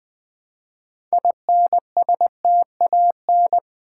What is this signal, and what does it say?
Morse code, heard as a keyed tone, about 20 wpm, 700 Hz. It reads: INSTAN
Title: Build A Word - 3 Letters At A Time - 8-Letter Words - 20wpm